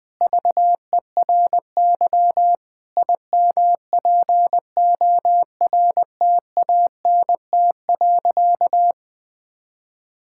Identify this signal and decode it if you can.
Morse code, heard as a keyed tone, about 20 wpm, 700 Hz. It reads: VERY IMPORTANT.
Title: Sentences with 4-Word Phrases - 20wpm